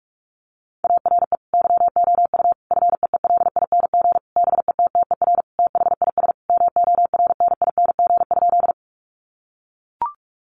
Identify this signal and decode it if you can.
Morse code, heard as a keyed tone, about 44 wpm, 700 Hz. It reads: ARE YOU FEELING BETTER THIS MORNING?
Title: Sentences from Top 1200 Words 45wpm